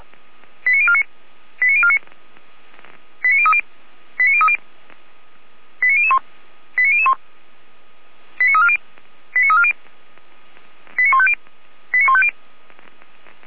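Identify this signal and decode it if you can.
Five-tone selective call (ZVEI), heard as quick runs of five tones, tones of about 70 ms. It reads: 89049 89049 89030 89030 89001 89001 89340 89340 89150 89150
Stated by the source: ZVEI five-tone selective call received on an SDR.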